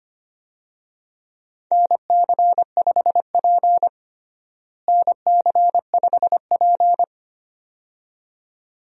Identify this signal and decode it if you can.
Morse code, heard as a keyed tone, about 25 wpm, 700 Hz. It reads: NC5P NC5P
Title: USA Call Signs - 4 Characters - Repeated 2x - 25wpm